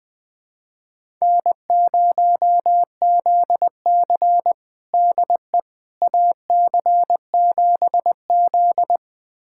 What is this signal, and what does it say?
Morse code, heard as a keyed tone, about 20 wpm, 700 Hz. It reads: N0ZC DE AC7Z
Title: USA Callsign DE Callsign - 4 Characters - 20wpm